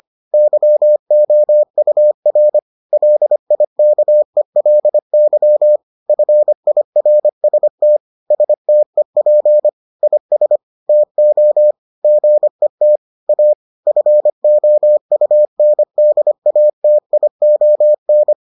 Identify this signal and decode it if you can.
Morse code, heard as a keyed tone, about 25 wpm, 600 Hz. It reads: YOUR LIKELY FIRST STEP IS TO GET A FOUNDATION